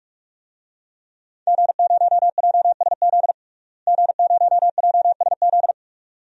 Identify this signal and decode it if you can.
Morse code, heard as a keyed tone, about 47 wpm, 700 Hz. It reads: G0JSZ G0JSZ
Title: DXCC-100 Call Signs - 5 Characters - Repeated 2x - 45wpm